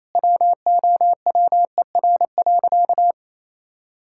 Morse code, heard as a keyed tone, about 28 wpm, 700 Hz. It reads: WOWER.
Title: Pangram Challenge v2 - 28wpm